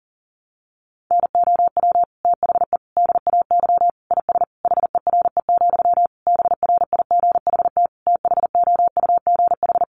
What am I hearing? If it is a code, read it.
Morse code, heard as a keyed tone, about 40 wpm, 700 Hz. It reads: NOW THE DAY IS HERE, BRIGHT THOUGH
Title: Sentences from Top 1000 Words 40wpm